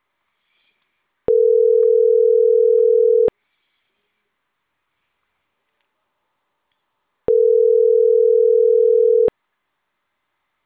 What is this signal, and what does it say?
Ringback tone (North America), 440 + 480 Hz.